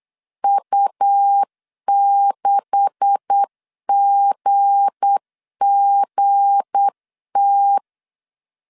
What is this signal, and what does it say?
Morse code, heard as a keyed tone, about 8 wpm, 800 Hz. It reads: U6GGT
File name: CW / DET / 1 9 1500C S A 8wpm 5637